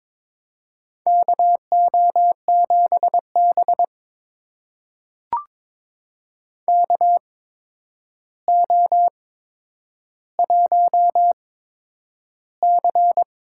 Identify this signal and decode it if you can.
Morse code, heard as a keyed tone, about 22 wpm, spaced as for 9 wpm, 700 Hz. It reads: KO7B KO1C
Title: USA Call Signs - 4 Characters - 8x Character Spacing - 22wpm